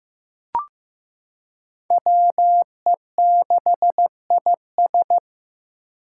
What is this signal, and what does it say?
Morse code, heard as a keyed tone, about 15 wpm, 700 Hz. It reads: WE6IS
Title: Five Challenging Letter-Number Combinations 15wpm